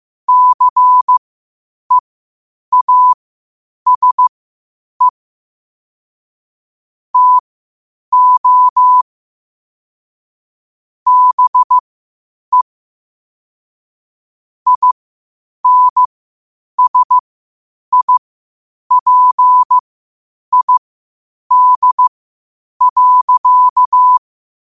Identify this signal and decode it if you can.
Morse code, heard as a keyed tone, about 15 wpm, spaced as for 8 wpm, 1000 Hz. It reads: CEASE TO BE INSIPID.
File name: qotdfiles / 8wpm